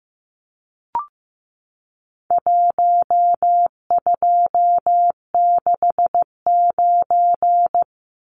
Morse code, heard as a keyed tone, about 14 wpm, 700 Hz. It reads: 1269
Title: Numbers - Contest Style - 4-Digit 15wpm